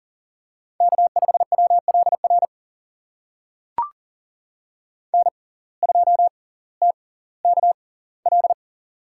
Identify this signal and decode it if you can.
Morse code, heard as a keyed tone, about 38 wpm, spaced as for 20 wpm, 700 Hz. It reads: K5WLR N2TKL
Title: USA Call Signs - 5 Characters - 6x Character Spacing - 40wpm